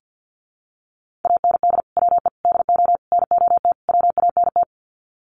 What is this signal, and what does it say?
Morse code, heard as a keyed tone, about 48 wpm, 700 Hz. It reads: AND WE DO NOT WANT
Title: Sets of 5 Words Encore2 50wpm